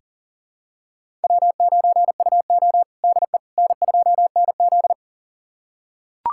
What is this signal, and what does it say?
Morse code, heard as a keyed tone, about 40 wpm, 700 Hz. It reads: W9UO DE N2NZ
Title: USA Callsign DE Callsign - 4 Characters - 40wpm